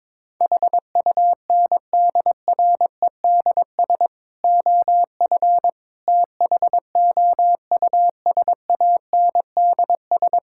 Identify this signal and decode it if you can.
Morse code, heard as a keyed tone, about 22 wpm, 700 Hz. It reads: HUNDREDS OF THOUSANDS
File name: practice / 22wpm / Sentences with 4-Word Phrases - 22wpm